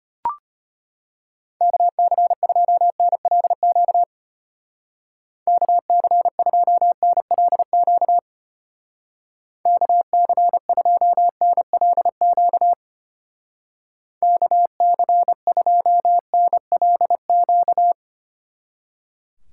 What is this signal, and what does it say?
Morse code, heard as a keyed tone, about 31 wpm, 700 Hz. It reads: KC2NLQ KC2NLQ KC2NLQ KC2NLQ
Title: Speed Racing - USA 2x3 Call Signs 25wpm